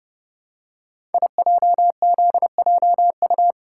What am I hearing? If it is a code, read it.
Morse code, heard as a keyed tone, about 30 wpm, 700 Hz. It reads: IJZJU
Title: Code Groups - 5 Letter:Numbers - 30wpm